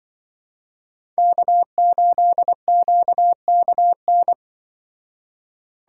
Morse code, heard as a keyed tone, about 25 wpm, 700 Hz. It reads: K8QKN
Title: Speed Racing - USA Call Signs 20wpm